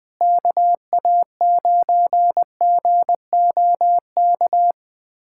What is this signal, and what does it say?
Morse code, heard as a keyed tone, about 20 wpm, 700 Hz. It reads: KA9GOK
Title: USA Callsign DE Callsign - 6 Characters - 20wpm